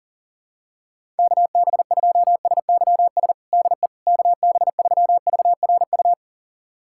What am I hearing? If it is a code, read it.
Morse code, heard as a keyed tone, about 40 wpm, 700 Hz. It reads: KB2SYS DE KB3VRU
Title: USA Callsign DE Callsign - 6 Characters - 40wpm